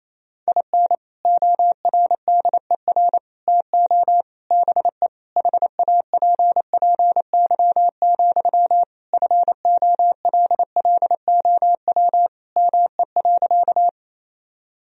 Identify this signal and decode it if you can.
Morse code, heard as a keyed tone, about 27 wpm, 700 Hz. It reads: IN ORDER TO BE HAPPY, FOLLOW ME.
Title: Sentences from Top 1000 Words - Review Part 2 - 28wpm